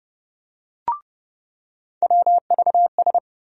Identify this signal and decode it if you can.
Morse code, heard as a keyed tone, about 30 wpm, 700 Hz. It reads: WVS
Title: USA Call Sign Postfixes - 3 Letters 30wpm